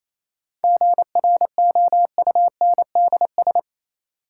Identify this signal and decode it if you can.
Morse code, heard as a keyed tone, about 28 wpm, 700 Hz. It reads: GROUNDS
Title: Sentences from Top 1300 Words - New Word Drill - 28wpm